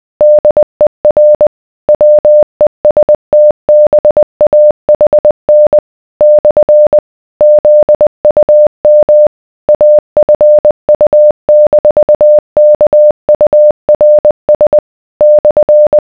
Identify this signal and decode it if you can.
Morse code, heard as a keyed tone, about 20 wpm, 600 Hz. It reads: DER WESTBAHN / ZUM AFU-KURS /